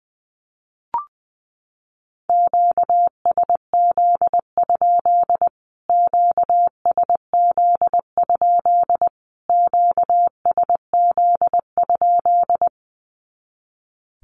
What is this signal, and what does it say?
Morse code, heard as a keyed tone, about 20 wpm, 700 Hz. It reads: QSZ? QSZ? QSZ?
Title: Q-Codes Repeated 3x 20wpm